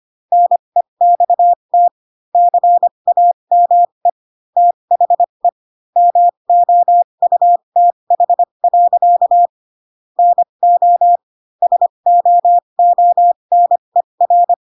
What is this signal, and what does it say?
Morse code, heard as a keyed tone, about 25 wpm, spaced as for 22 wpm, 700 Hz. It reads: NEXT CAME THE MOUTH. NO SOONER
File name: Pinocchio / Pinocchio ch3 18 25 ik1hge